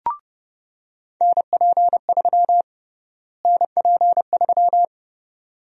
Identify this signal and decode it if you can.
Morse code, heard as a keyed tone, about 32 wpm, 700 Hz. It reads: NP3 NP3
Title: USA Call Sign Prefixes - Repeated 2x - 30wpm